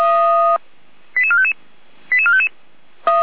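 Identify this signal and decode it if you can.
Five-tone selective call (ZVEI), heard as quick runs of five tones, tones of about 70 ms. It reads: 88455 88455
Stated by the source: Five-tone ZVEI selective call 88102, received by SDR.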